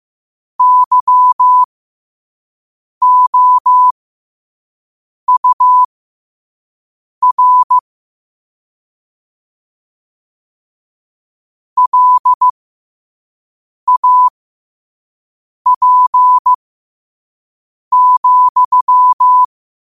Morse code, heard as a keyed tone, about 15 wpm, spaced as for 5 wpm, 1000 Hz. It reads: YOUR LAP,
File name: qotdfiles / 5wpm